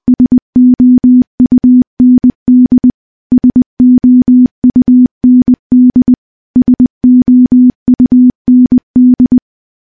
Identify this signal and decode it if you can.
Morse code, heard as a keyed tone, about 20 wpm, 250 Hz. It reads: SOUND SOUND SOUND
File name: morse / morse-training-20WPM